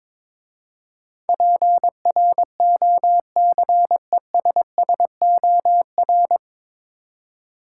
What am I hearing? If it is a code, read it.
Morse code, heard as a keyed tone, about 22 wpm, 700 Hz. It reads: PROCESSOR
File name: 9 Letter Words - First 3 Letters Given - 22wpm